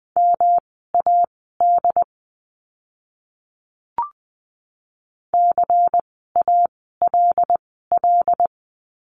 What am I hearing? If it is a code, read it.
Morse code, heard as a keyed tone, about 20 wpm, spaced as for 15 wpm, 700 Hz. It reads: MAD CALL
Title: 108 Lesson 18f - Letter m - Words 20wpm